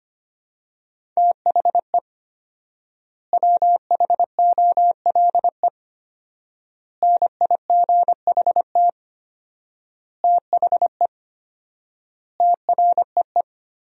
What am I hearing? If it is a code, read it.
Morse code, heard as a keyed tone, about 25 wpm, 700 Hz. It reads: THE WHOLE NIGHT THE TREE
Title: 4x Word-Spacing - Sentences from Top 500 Words - 25wpm